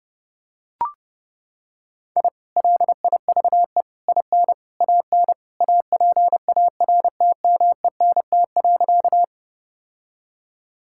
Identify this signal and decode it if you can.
Morse code, heard as a keyed tone, about 30 wpm, 700 Hz. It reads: I LIVE IN AN APARTMENT.
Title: Sentences from Top 1000 Words - Review Part 6 - 30wpm